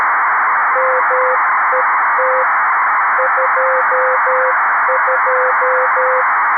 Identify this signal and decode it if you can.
Morse code, heard as a keyed tone, about 13 wpm, 500 Hz, with noise. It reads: MET 22
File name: arissat / recordings / ARISSat 11Sept2011 1354utc 60frames